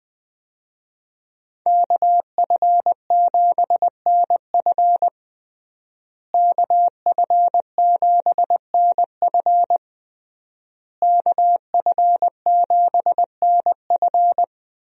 Morse code, heard as a keyed tone, about 20 wpm, 700 Hz. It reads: KF7NF KF7NF KF7NF
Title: USA Call Signs - 5 Characters - Repeated 3x - 20wpm